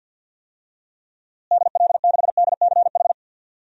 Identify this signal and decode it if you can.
Morse code, heard as a keyed tone, about 52 wpm, 700 Hz. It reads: DL6DCH
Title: DXCC-100 Call Signs - 6 Characters - Repeated 3x - 50wpm